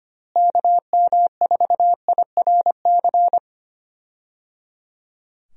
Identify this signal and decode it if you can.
Morse code, heard as a keyed tone, about 25 wpm, 700 Hz. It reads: KM4IRC
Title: USA Call Signs - 6 Characters - Repeated 3x - 25wpm